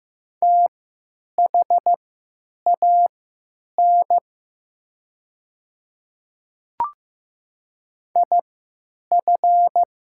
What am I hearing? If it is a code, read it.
Morse code, heard as a keyed tone, about 15 wpm, spaced as for 9 wpm, 700 Hz. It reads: THAN IF
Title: 3x Character Spacing - Top 100 Words - 15wpm